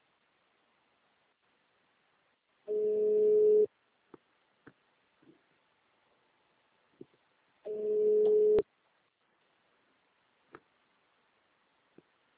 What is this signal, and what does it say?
Ringback tone (Europe, 425 Hz).